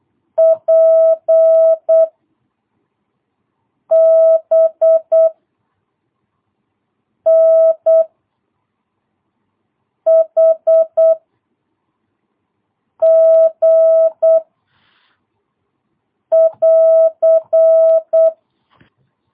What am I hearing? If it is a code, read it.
Morse code, heard as a keyed tone, about 8 wpm, 660 Hz. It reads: P B N H G <AR>
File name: jh8cbh / 2022cw